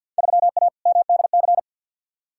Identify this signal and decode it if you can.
Morse code, heard as a keyed tone, about 53 wpm, 700 Hz. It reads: 3A MDC
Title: Field Day Exchanges - 50wpm